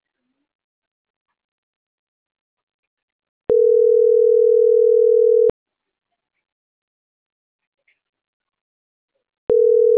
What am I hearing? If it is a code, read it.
Ringback tone (North America), 440 + 480 Hz.